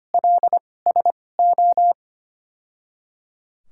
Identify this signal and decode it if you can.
Morse code, heard as a keyed tone, about 25 wpm, spaced as for 18 wpm, 700 Hz. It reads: LSO
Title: 2x Character Spacing - Top 300 Words - 25wpm